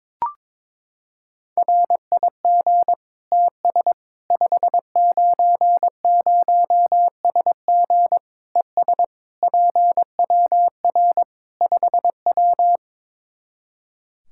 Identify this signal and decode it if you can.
Morse code, heard as a keyed tone, about 22 wpm, 700 Hz. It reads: RIG TS 590SG ES PWR 5W
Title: Practice QSO - Exchange 2 - Info - 22wpm